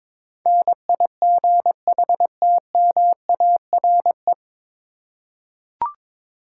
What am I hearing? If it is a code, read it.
Morse code, heard as a keyed tone, about 23 wpm, 700 Hz. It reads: NIGHTMARE
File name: 9 Letter Words - First 3 Letters Given - 22wpm